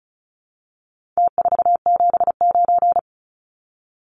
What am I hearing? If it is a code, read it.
Morse code, heard as a keyed tone, about 34 wpm, 700 Hz. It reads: T479
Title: Numbers - Contest Style - 4-Digit 35wpm